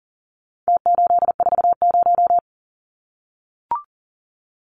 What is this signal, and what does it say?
Morse code, heard as a keyed tone, about 40 wpm, 700 Hz. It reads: T840
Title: Numbers - Contest Style - 4-Digit 40wpm